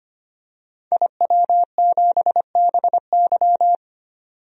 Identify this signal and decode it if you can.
Morse code, heard as a keyed tone, about 25 wpm, 700 Hz. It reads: IW7BY
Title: DXCC-100 Call Signs - 5 Characters - 25wpm